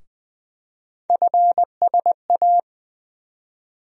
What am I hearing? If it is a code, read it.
Morse code, heard as a keyed tone, about 20 wpm, 700 Hz. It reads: FSA
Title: USA Call Sign Postfixes - 3 Letters 20wpm